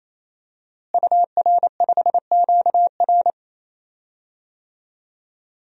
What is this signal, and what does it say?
Morse code, heard as a keyed tone, about 27 wpm, 700 Hz. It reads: UR5QR
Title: DXCC-100 Call Signs - 5 Characters - Repeated 3x - 28wpm